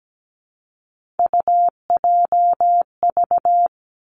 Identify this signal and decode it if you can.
Morse code, heard as a keyed tone, about 17 wpm, 700 Hz. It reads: UJV
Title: Three Challenging Letter-Number Combinations 17wpm